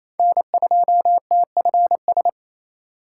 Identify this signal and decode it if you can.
Morse code, heard as a keyed tone, about 28 wpm, 700 Hz. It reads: N2TFS
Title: USA Call Signs - 5 Characters - 4x Character Spacing - 28wpm